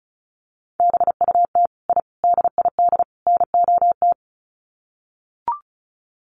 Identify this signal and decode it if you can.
Morse code, heard as a keyed tone, about 34 wpm, 700 Hz. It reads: BUT I DID NOT
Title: Sets of 4 Words 35wpm